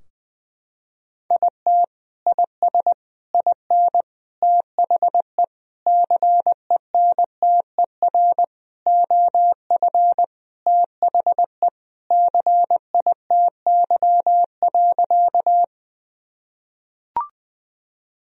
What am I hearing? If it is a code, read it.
Morse code, heard as a keyed tone, about 21 wpm, 700 Hz. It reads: IT IS IN THE CENTER OF THE CITY.
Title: Sentences from Top 1000 Words - Review Part 2 - 20wpm